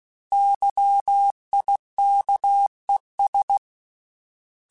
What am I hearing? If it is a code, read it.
Morse code, heard as a keyed tone, about 16 wpm, 800 Hz. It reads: YIKES
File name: Misc / Misc-mp3s / Misc @15